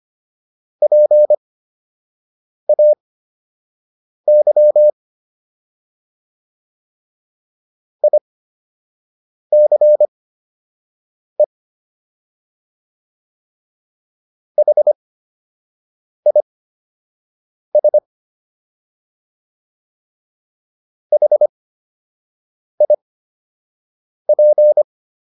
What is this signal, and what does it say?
Morse code, heard as a keyed tone, about 25 wpm, spaced as for 6 wpm, 600 Hz. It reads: PAY ICE HIS HIP